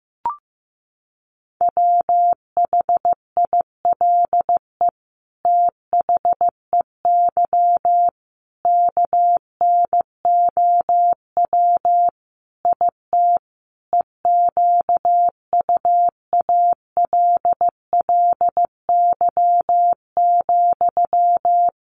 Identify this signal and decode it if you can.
Morse code, heard as a keyed tone, about 15 wpm, 700 Hz. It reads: WHILE THEY KNOW IT EQUALLY,
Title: Sentences from Top 1400 Words 15wpm